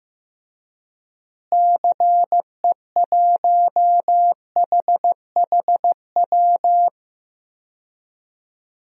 Morse code, heard as a keyed tone, about 15 wpm, 700 Hz. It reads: CE1HHW
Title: DXCC-100 Call Signs - 6 Characters - Repeated 3x - 15wpm